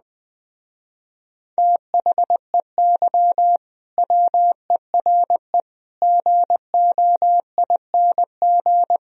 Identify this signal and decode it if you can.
Morse code, heard as a keyed tone, about 20 wpm, 700 Hz. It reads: THEY WERE GOING
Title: Speed-Racing - Sets of 4 Words 17wpm